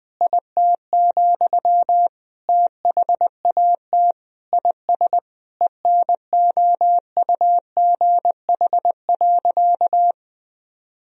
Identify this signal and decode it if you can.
Morse code, heard as a keyed tone, about 21 wpm, 700 Hz. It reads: IT, THAT IS ENOUGH.
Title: Sentences from Top 400 Words and Abbreviations 20wpm